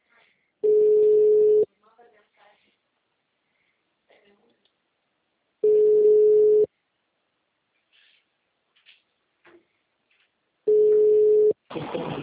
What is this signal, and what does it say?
Ringback tone (Europe, 425 Hz), with speech.
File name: out- -2034- - - .54479